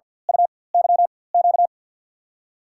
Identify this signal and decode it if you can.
Morse code, heard as a keyed tone, about 53 wpm, spaced as for 23 wpm, 700 Hz. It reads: UYQ
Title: USA Call Signs - 5 Characters - 4x Character Spacing - 50wpm